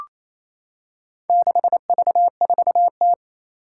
Morse code, heard as a keyed tone, about 29 wpm, 700 Hz. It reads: 6V4T
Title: Four Challenging Letter-Number Combinations 28wpm